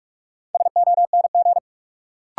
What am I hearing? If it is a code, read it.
Morse code, heard as a keyed tone, about 46 wpm, 700 Hz. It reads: SONG
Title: Sentences from Top 400 Words - New Word Drill - 45wpm